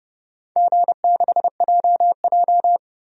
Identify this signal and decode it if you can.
Morse code, heard as a keyed tone, about 30 wpm, 700 Hz. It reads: G6JJ
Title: DXCC-100 Call Signs - 4 Characters - 30wpm